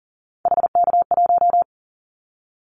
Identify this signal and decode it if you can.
Morse code, heard as a keyed tone, about 40 wpm, 700 Hz. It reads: HK1
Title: Callsign Prefixes for Top 100 DXCC - 40wpm